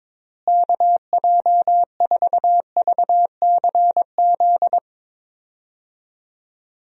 Morse code, heard as a keyed tone, about 22 wpm, 700 Hz. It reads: KJ4VCZ
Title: USA Call Signs - 6 Characters - Repeated 2x - 22wpm